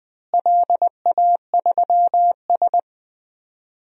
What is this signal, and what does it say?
Morse code, heard as a keyed tone, about 20 wpm, 700 Hz. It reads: LA3S
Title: DXCC-100 Call Signs - 4 Characters - Repeated 2x - 20wpm